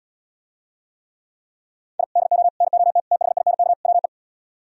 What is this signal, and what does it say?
Morse code, heard as a keyed tone, about 76 wpm, 700 Hz. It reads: I DO NOT THINK WE